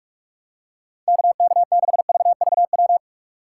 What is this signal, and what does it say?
Morse code, heard as a keyed tone, about 46 wpm, 700 Hz. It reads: KK6VVW
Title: USA Call Signs - 3x Character Spacing - 45wpm